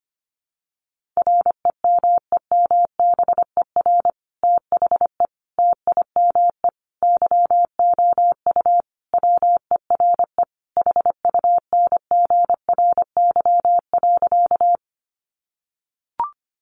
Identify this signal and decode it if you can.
Morse code, heard as a keyed tone, about 25 wpm, 700 Hz. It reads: REMEMBER THE TIME YOU WERE HUNGRY.
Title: Sentences from Top 1400 Words 25wpm